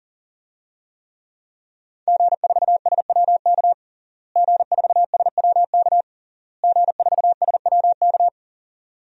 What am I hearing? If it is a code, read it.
Morse code, heard as a keyed tone, about 40 wpm, 700 Hz. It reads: G4SWK G4SWK G4SWK
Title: DXCC-100 Call Signs - 5 Characters - Repeated 3x - 40wpm